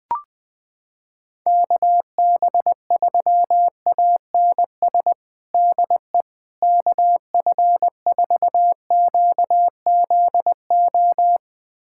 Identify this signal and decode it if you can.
Morse code, heard as a keyed tone, about 21 wpm, 700 Hz. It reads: KB3ANS DE KF4QZO
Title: USA Callsign DE Callsign - 6 Characters - 20wpm